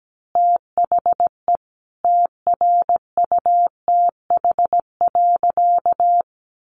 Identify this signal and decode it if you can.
Morse code, heard as a keyed tone, about 17 wpm, 700 Hz. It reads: THE TRUTH.
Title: Sentences from Top 1100 Words 17wpm